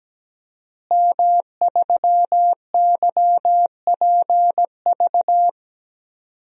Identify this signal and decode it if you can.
Morse code, heard as a keyed tone, about 17 wpm, 700 Hz. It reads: M3YPV
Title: DXCC-100 Call Signs - 5 Characters - Repeated 2x - 17wpm